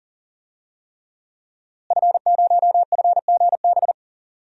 Morse code, heard as a keyed tone, about 40 wpm, 700 Hz. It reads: F0FGB
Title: DXCC-100 Call Signs - 5 Characters - Repeated 3x - 40wpm